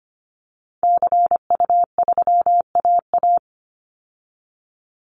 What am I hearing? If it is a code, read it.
Morse code, heard as a keyed tone, about 25 wpm, 700 Hz. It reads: CU3AA
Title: Call Signs Repeated 3x - 25wpm